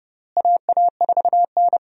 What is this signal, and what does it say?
Morse code, heard as a keyed tone, about 32 wpm, 700 Hz. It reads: AA4N
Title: USA Call Signs - 4 Characters - Repeated 3x - 30wpm